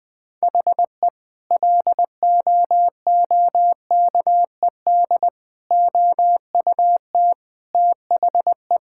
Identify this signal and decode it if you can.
Morse code, heard as a keyed tone, about 20 wpm, 700 Hz. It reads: HE LOOKED OUT THE
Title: Sentences with 2-Word Phrases - 20wpm